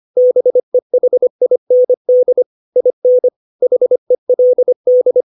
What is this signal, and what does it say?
Morse code, heard as a keyed tone, about 25 wpm, 500 Hz. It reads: BEHIND IN HELD
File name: 632 words / 632 Words 25WPM